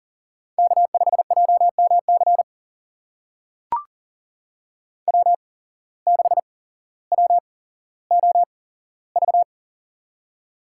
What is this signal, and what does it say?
Morse code, heard as a keyed tone, about 38 wpm, spaced as for 17 wpm, 700 Hz. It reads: K5JMC W6WOV
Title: USA Call Signs - 5 Characters - 8x Character Spacing - 40wpm